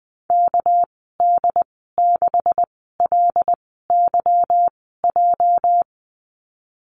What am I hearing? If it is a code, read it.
Morse code, heard as a keyed tone, about 20 wpm, spaced as for 14 wpm, 700 Hz. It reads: KD6LYJ
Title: 213 Lesson 34g - Number 8 - Call Signs 20wpm